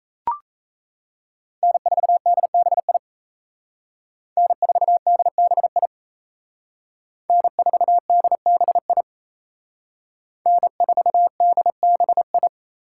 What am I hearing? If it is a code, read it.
Morse code, heard as a keyed tone, about 36 wpm, 700 Hz. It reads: N4DBI N4DBI N4DBI N4DBI